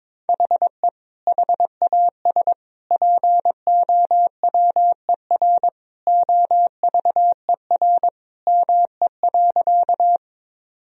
Morse code, heard as a keyed tone, about 22 wpm, 700 Hz. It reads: HE HAS POWER OVER ME.
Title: Sentences from Top 1000 Words - Review Part 2 - 22wpm